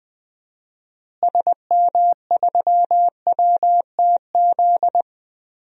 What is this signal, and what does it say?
Morse code, heard as a keyed tone, about 20 wpm, 700 Hz. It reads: SM3WTZ
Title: DXCC-100 Call Signs - 6 Characters - Repeated 2x - 20wpm